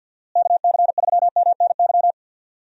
Morse code, heard as a keyed tone, about 52 wpm, 700 Hz. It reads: KC3MNY
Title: USA Call Signs - 6 Characters - Repeated 2x - 50wpm